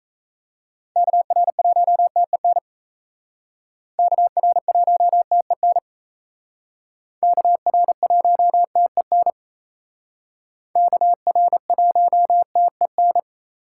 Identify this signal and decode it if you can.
Morse code, heard as a keyed tone, about 36 wpm, 700 Hz. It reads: KR1TEN KR1TEN KR1TEN KR1TEN